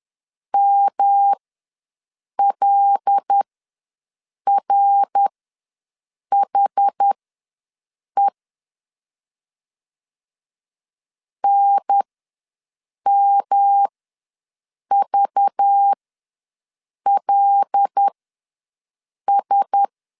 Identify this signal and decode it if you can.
Morse code, heard as a keyed tone, about 10 wpm, spaced as for 6 wpm, 800 Hz. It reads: MLRHE NMVLS
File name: LEZIONI / 5 LBVC / 5 LBV ws 10wpm 3645